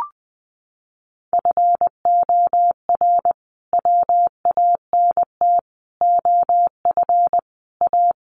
Morse code, heard as a keyed tone, about 19 wpm, 700 Hz. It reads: FOR WANT OF A
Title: A Sound Of Thunder 20wpm